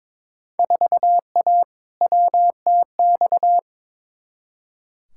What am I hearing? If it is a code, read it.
Morse code, heard as a keyed tone, about 22 wpm, 700 Hz. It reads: 4A WTX
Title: Field Day Exchanges Encore - 22wpm